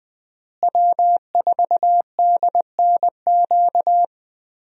Morse code, heard as a keyed tone, about 19 wpm, 700 Hz. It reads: W4DNQ
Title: USA Call Signs - 5 Characters - Repeated 3x - 20wpm